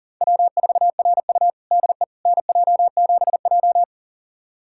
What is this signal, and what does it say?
Morse code, heard as a keyed tone, about 40 wpm, 700 Hz. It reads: W4RU DE NJ7J
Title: USA Callsign DE Callsign - 4 Characters - 40wpm